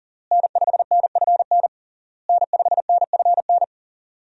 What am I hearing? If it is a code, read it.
Morse code, heard as a keyed tone, about 40 wpm, 700 Hz. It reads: N5NFN N5NFN
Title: USA Call Signs - 5 Characters - Repeated 3x - 40wpm